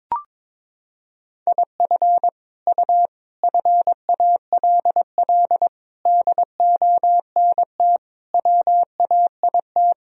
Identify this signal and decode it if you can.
Morse code, heard as a keyed tone, about 22 wpm, 700 Hz. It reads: IF U FALL DONT WAIT
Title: Sentences from Top 500 Words and Abbreviations 22wpm